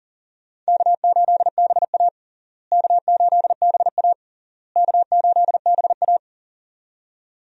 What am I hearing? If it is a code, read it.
Morse code, heard as a keyed tone, about 40 wpm, 700 Hz. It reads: K8BA K8BA K8BA
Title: USA Call Signs - 4 Characters - Repeated 3x - 40wpm